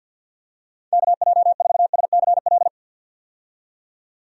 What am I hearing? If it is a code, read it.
Morse code, heard as a keyed tone, about 47 wpm, 700 Hz. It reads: KJ4SCL
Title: USA Call Signs - 6 Characters - 50wpm